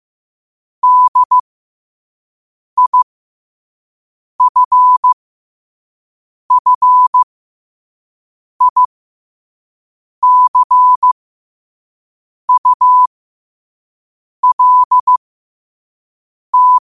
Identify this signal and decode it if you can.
Morse code, heard as a keyed tone, about 15 wpm, spaced as for 5 wpm, 1000 Hz. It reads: DIFFICULT
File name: qotdfiles / 5wpm